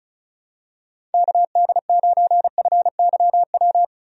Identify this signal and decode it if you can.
Morse code, heard as a keyed tone, about 34 wpm, 700 Hz. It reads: KD9FYW
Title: USA Call Signs - 6 Characters - 35wpm